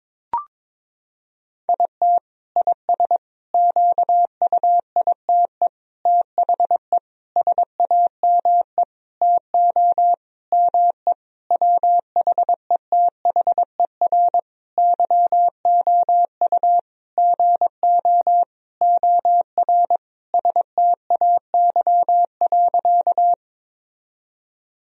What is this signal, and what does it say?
Morse code, heard as a keyed tone, about 22 wpm, 700 Hz. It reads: IT IS QUITE THE SAME TO ME WHETHER YOU GO OR STAY.
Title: Sentences from Top 1000 Words - Review Part 3 - 22wpm